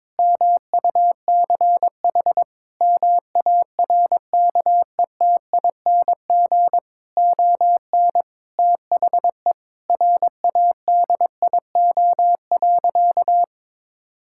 Morse code, heard as a keyed tone, about 22 wpm, 700 Hz. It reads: MUCH MARKETING ON THE RADIO.
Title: Sentences from Top 1000 Words - Review Part 5 - 22wpm